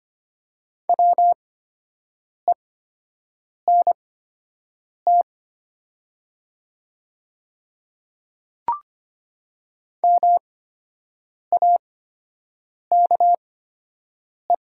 Morse code, heard as a keyed tone, about 25 wpm, spaced as for 7 wpm, 700 Hz. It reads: WENT MAKE
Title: 8x Character Spacing - Top 200 Words - 25wpm